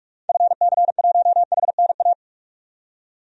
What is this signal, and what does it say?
Morse code, heard as a keyed tone, about 44 wpm, 700 Hz. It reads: FC1HNA
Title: DXCC-100 Call Signs - 6 Characters - 45wpm